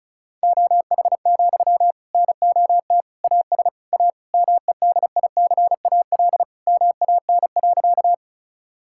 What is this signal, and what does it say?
Morse code, heard as a keyed tone, about 36 wpm, 700 Hz. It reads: OH, NOT AS A MEDICAL MAN.
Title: Sentences from Top 1000 Words - Review Part 5 - 35wpm